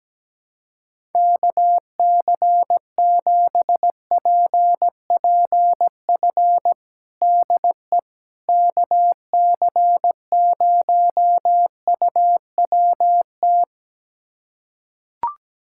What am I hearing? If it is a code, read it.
Morse code, heard as a keyed tone, about 17 wpm, 700 Hz. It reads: KC7PPF DE KC0UWT
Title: USA Callsign DE Callsign - 6 Characters - 17wpm